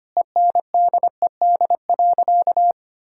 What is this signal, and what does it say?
Morse code, heard as a keyed tone, about 25 wpm, 700 Hz. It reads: ENDED.
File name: practice / 25wpm / Sentences with 2-Word Phrases - 25wpm